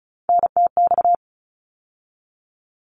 Morse code, heard as a keyed tone, about 34 wpm, 700 Hz. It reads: NTX
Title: ARRL Sections 35wpm